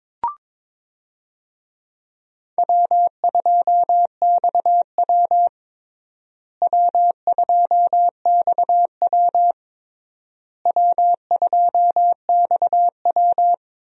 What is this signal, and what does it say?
Morse code, heard as a keyed tone, about 22 wpm, 700 Hz. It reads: W2XW W2XW W2XW
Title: USA Call Signs - 4 Characters - Repeated 3x - 22wpm